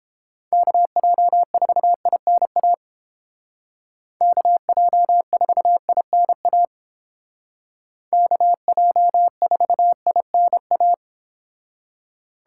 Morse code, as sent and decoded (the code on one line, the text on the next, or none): -.- .--- ....- .. -. .- / -.- .--- ....- .. -. .- / -.- .--- ....- .. -. .-
KJ4INA KJ4INA KJ4INA